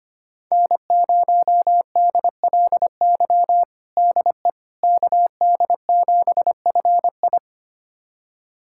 Morse code, heard as a keyed tone, about 25 wpm, 700 Hz. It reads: N0DLY DE KD7FI
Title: USA Callsign DE Callsign - 5 Characters - 25wpm